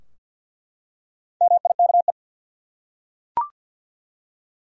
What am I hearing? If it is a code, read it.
Morse code, heard as a keyed tone, about 46 wpm, 700 Hz. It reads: MIKE
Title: USA Common Names - Max 5 Letters - 50wpm